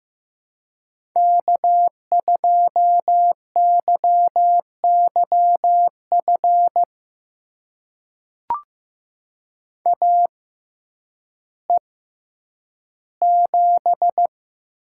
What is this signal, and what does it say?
Morse code, heard as a keyed tone, about 15 wpm, spaced as for 9 wpm, 700 Hz. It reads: K2YYF AE7
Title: USA Call Signs - 5 Characters - 6x Character Spacing - 15wpm